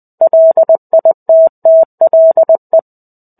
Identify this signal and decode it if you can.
Morse code, heard as a keyed tone, about 20 wpm, 650 Hz. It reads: LITTLE
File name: 05-Chapter-05 20 WPM